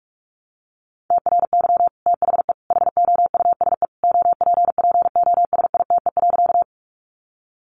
Morse code, heard as a keyed tone, about 44 wpm, 700 Hz. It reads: TRY THE HOUSE OPPOSITE.
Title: Sentences from Top 1300 Words 45wpm